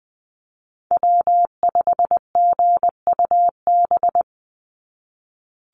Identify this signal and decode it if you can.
Morse code, heard as a keyed tone, about 20 wpm, 700 Hz. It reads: W5GUB
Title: USA Call Signs 20wpm